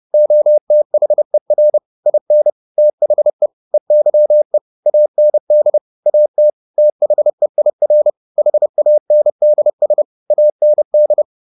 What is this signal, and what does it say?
Morse code, heard as a keyed tone, about 30 wpm, 600 Hz. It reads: OTHER IN THE EYE AND AT THEIR HANDS AND
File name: the old man and the sea 2 30wpm